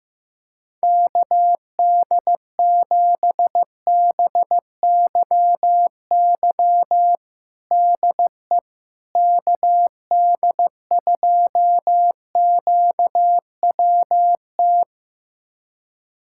Morse code, heard as a keyed tone, about 15 wpm, 700 Hz. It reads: KD7BYY DE KD2QWT
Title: USA Callsign DE Callsign - 6 Characters - 15wpm